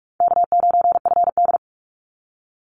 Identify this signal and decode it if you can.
Morse code, heard as a keyed tone, about 44 wpm, 700 Hz. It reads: K9FD
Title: Easy Call Signs Repeated 2x - 45wpm